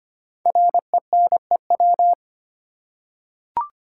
Morse code, heard as a keyed tone, about 25 wpm, 700 Hz. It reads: RENEW
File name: 5 Letter Words - First 3 Letters Given - 25wpm